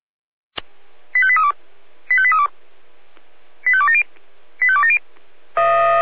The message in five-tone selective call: 85832 85832 85270 85270

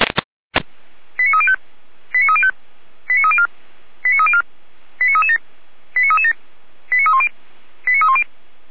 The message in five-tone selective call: 89305 89305 89304 89304 89337 89337 89310 89310

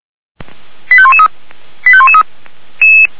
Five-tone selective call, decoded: 85113 85113

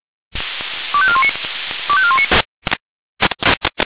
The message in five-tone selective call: 25419 25419

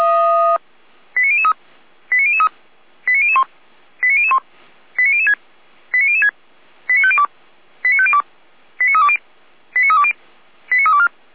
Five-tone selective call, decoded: 89003 89003 89001 89001 89006 89006 89502 89502 89320 89320 89324

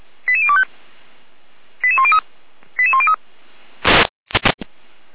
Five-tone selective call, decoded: 88025 88103 88103

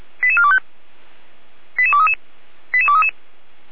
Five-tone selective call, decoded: 88425 88233 88233

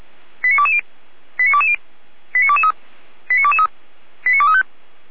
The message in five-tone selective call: 89220 89220 89223 89223 89235

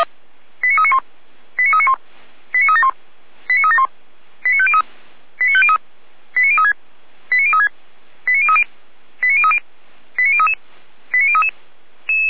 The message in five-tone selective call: 89391 89391 89371 89371 89503 89503 89036 89036 89030 89030 89033 89033

